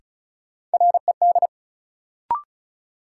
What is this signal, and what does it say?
Morse code, heard as a keyed tone, about 34 wpm, 700 Hz. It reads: RED
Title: Sentences from Top 700 Words - New Word Drill - 35wpm